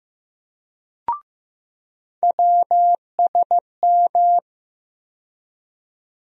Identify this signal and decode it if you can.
Morse code, heard as a keyed tone, about 15 wpm, 700 Hz. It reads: WSM
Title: USA Call Sign Postfixes - 3 Letters 15wpm